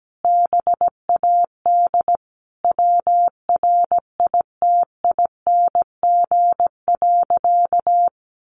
Morse code, heard as a keyed tone, about 17 wpm, 700 Hz. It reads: BAD WRITING.
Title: Sentences from Top 800 Words 17wpm